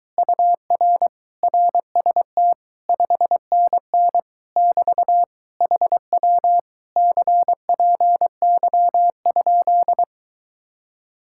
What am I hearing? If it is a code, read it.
Morse code, heard as a keyed tone, about 23 wpm, 700 Hz. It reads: UR RST 5NN <BT> HW CPY?